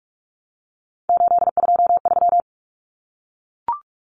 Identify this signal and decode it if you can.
Morse code, heard as a keyed tone, about 44 wpm, 700 Hz. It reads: 823
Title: Common Cut Numbers - 3-Digit 45wpm